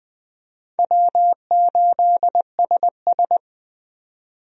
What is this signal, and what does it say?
Morse code, heard as a keyed tone, about 20 wpm, 700 Hz. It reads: W8SS
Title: USA Call Signs - 4 Characters - 20wpm